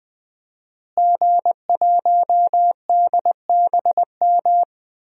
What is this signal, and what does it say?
Morse code, heard as a keyed tone, about 20 wpm, 700 Hz. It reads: G1DBM
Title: DXCC-100 Call Signs - 5 Characters - Repeated 3x - 20wpm